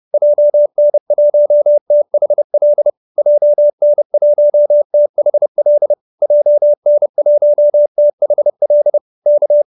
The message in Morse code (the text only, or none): JN1THL JN1THL JN1THL K